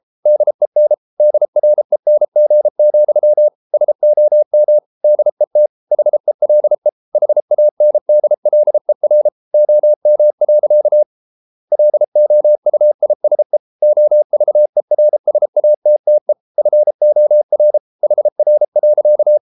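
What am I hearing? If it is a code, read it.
Morse code, heard as a keyed tone, about 33 wpm, 600 Hz. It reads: DEN DRENG, SOM DET HELE HANDLER OM. LOUISE OVERSATTE FOR HR.
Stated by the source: Morse code at 33 wpm, QRQ.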